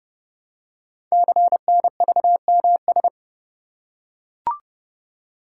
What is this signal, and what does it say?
Morse code, heard as a keyed tone, about 30 wpm, 700 Hz. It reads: CNVMS
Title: Code Groups - 5 Letters - 30wpm